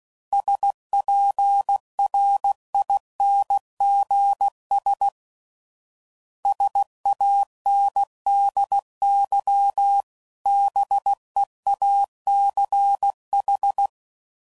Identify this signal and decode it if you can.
Morse code, heard as a keyed tone, about 16 wpm, 800 Hz. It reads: SPRINGS SANDY BEACH
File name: CW Code Groups @15WPM / fromSM's / Cities